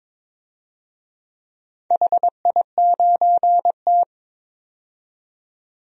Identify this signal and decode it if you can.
Morse code, heard as a keyed tone, about 23 wpm, 690 Hz. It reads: HI9T
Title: DXCC-100 Call Signs - 4 Characters - 22wpm